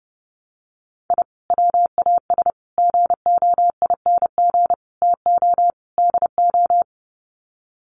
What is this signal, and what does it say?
Morse code, heard as a keyed tone, about 30 wpm, 700 Hz. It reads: I WAS GOING TO DO
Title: Sets of 5 Words Encore2 30wpm